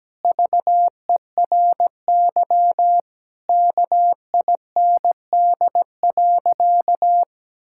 Morse code, heard as a keyed tone, about 17 wpm, 700 Hz. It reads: VERY KIND.
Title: Sentences from Top 1600 Words 17wpm